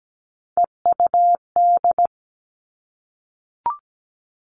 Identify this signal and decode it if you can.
Morse code, heard as a keyed tone, about 17 wpm, 700 Hz. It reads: EUD
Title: Three Challenging Letter-Number Combinations 17wpm